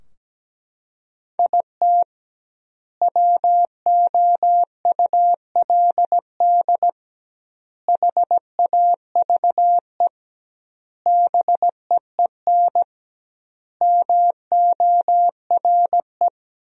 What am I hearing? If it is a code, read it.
Morse code, heard as a keyed tone, about 17 wpm, 700 Hz. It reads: IT WOULD HAVE BEEN MORE
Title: 2x Word-Spacing - Sentences from Top 400 Words - 17wpm